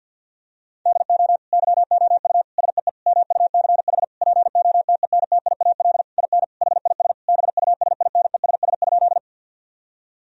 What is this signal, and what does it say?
Morse code, heard as a keyed tone, about 50 wpm, 700 Hz. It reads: DO YOU SEE MUCH POTENTIAL IN HIS BUSINESS?
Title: Sentences from Top 1000 Words - Review Part 4 - 50wpm